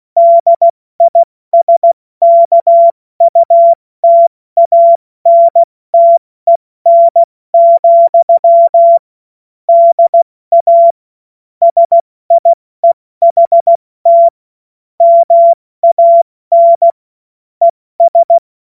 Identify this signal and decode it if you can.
Morse code, heard as a keyed tone, about 16 wpm, 680 Hz. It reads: DISKUTANTEN, DA SIEHT MAN ES
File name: Morsekurs / Witze / Witz 10.Oktober 2023 70 F80